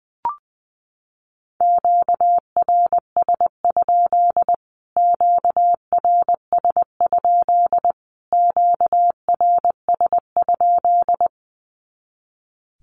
Morse code, heard as a keyed tone, about 20 wpm, 700 Hz. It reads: QRS? QRS? QRS?
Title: Q-Codes Repeated 3x 20wpm